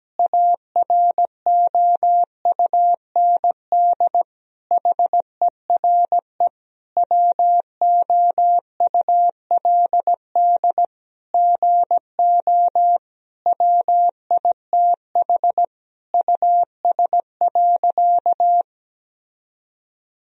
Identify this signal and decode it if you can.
Morse code, heard as a keyed tone, about 17 wpm, 700 Hz. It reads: AROUND HERE WOULD GO WITH US.
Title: Sentences from Top 1000 Words - Review Part 3 - 17wpm